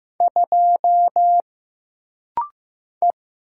Morse code, heard as a keyed tone, about 15 wpm, 700 Hz. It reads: 2 E
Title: Single Letter-Number Alt - Rapid-Fire 15wpm